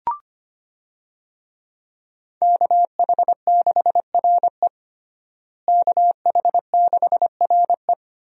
Morse code, heard as a keyed tone, about 25 wpm, 700 Hz. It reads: KH6RE KH6RE
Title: USA Call Signs - 5 Characters - Repeated 2x - 25wpm